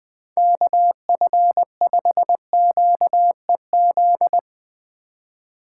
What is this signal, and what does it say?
Morse code, heard as a keyed tone, about 20 wpm, 700 Hz. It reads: KF5QEZ
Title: Speed Racing - USA 2x3 Call Signs 17wpm